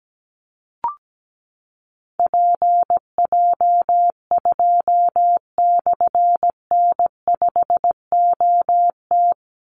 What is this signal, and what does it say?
Morse code, heard as a keyed tone, about 17 wpm, 700 Hz. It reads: PJ2/N5OT
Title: Call Signs - 17wpm